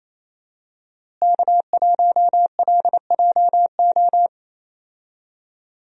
Morse code, heard as a keyed tone, about 28 wpm, 700 Hz. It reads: K1LJO
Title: USA Call Signs - 5 Characters - Repeated 2x - 28wpm